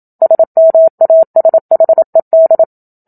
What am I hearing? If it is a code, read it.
Morse code, heard as a keyed tone, about 27 wpm, 650 Hz. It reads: SMASHED
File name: Book Two / 01-Chapter 1 - 27 WPM